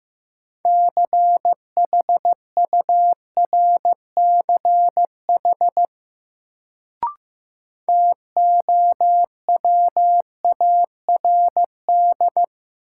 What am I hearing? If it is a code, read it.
Morse code, heard as a keyed tone, about 15 wpm, 700 Hz. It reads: CHURCH TOWARD
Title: Sets of 5 Words - New Word Drill - 15wpm